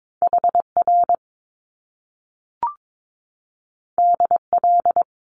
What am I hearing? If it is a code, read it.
Morse code, heard as a keyed tone, about 22 wpm, 700 Hz. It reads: HR DL
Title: Top Two Letter Combinations in Words 22wpm